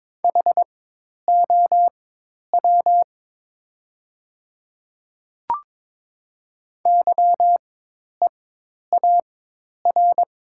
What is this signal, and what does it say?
Morse code, heard as a keyed tone, about 22 wpm, spaced as for 10 wpm, 700 Hz. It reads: HOW YEAR
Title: 4x Character Spacing - Top 300 Words - 22wpm